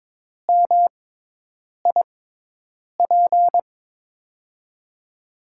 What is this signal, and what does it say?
Morse code, heard as a keyed tone, about 21 wpm, spaced as for 8 wpm, 700 Hz. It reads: MIP
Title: USA Call Signs - 6 Characters - 6x Character Spacing - 22wpm